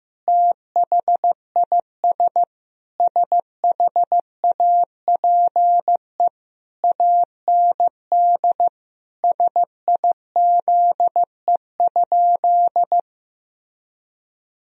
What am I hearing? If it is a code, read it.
Morse code, heard as a keyed tone, about 15 wpm, 700 Hz. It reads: THIS SHAPE AND SIZE?
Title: Sentences from Top 1000 Words - Review Part 3 - 15wpm